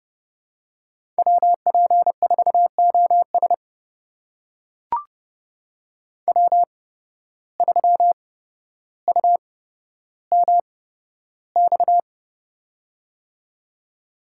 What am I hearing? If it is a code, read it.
Morse code, heard as a keyed tone, about 30 wpm, spaced as for 13 wpm, 700 Hz. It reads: WP4OS W3UMX
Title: USA Call Signs - 5 Characters - 8x Character Spacing - 30wpm